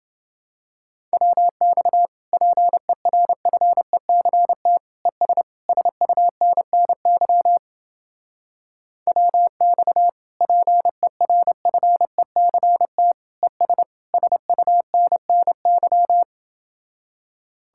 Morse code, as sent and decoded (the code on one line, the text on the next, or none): .-- -..- / .--. .-. ..-. -.-. - / . ... / ... ..- -. -. -.-- / .-- -..- / .--. .-. ..-. -.-. - / . ... / ... ..- -. -. -.--
WX PERFECT ES SUNNY WX PERFECT ES SUNNY